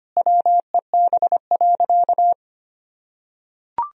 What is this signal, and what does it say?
Morse code, heard as a keyed tone, about 25 wpm, 700 Hz. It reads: WEB.
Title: Sentences from Top 1000 Words - Review Part 4 - 25wpm